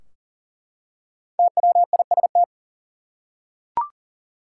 Morse code, heard as a keyed tone, about 37 wpm, 700 Hz. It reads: TWIST
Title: Five Letter Words - 40wpm